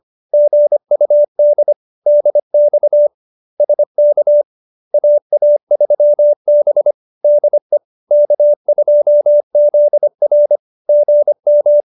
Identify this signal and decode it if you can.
Morse code, heard as a keyed tone, about 25 wpm, 570 Hz. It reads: GUD DX SK AA3B DE K2ZR GM